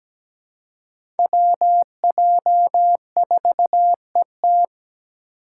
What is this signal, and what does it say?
Morse code, heard as a keyed tone, about 17 wpm, 700 Hz. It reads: WJ4ET
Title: USA Call Signs - 5 Characters - 4x Character Spacing - 17wpm